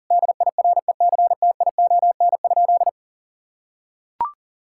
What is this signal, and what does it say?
Morse code, heard as a keyed tone, about 40 wpm, 700 Hz. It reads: DIRECTION?
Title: Sentences from Top 1000 Words - Review Part 5 - 40wpm